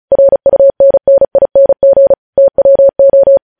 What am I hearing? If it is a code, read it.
Morse code, heard as a keyed tone, about 36 wpm, 550 Hz. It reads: RUNNING TWO
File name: podcast / News headlines-1x-35x35